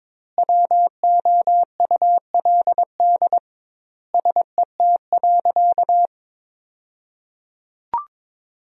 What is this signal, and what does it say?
Morse code, heard as a keyed tone, about 22 wpm, 700 Hz. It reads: WOULD SET.
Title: 2x Word-Spacing - Sentences from Top 400 Words - 22wpm